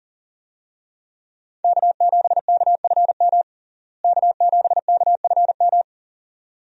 Morse code, heard as a keyed tone, about 40 wpm, 700 Hz. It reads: K7KFM K7KFM
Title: USA Call Signs - 5 Characters - Repeated 2x - 40wpm